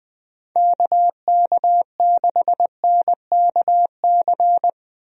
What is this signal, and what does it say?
Morse code, heard as a keyed tone, about 19 wpm, 700 Hz. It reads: KK6NKC
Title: USA Call Signs - 6 Characters - Repeated 3x - 20wpm